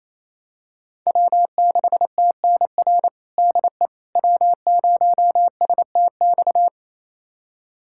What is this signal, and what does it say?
Morse code, heard as a keyed tone, about 28 wpm, 700 Hz. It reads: W6TNR DE W0STX
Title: USA Callsign DE Callsign - 5 Characters - 28wpm